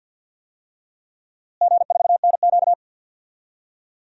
Morse code, heard as a keyed tone, about 53 wpm, 700 Hz. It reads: G4NQ
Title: DXCC-100 Call Signs - 4 Characters - Repeated 2x - 50wpm